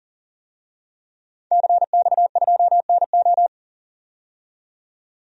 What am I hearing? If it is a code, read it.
Morse code, heard as a keyed tone, about 40 wpm, 700 Hz. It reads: CX2NO
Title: DXCC-100 Call Signs - 5 Characters - 40wpm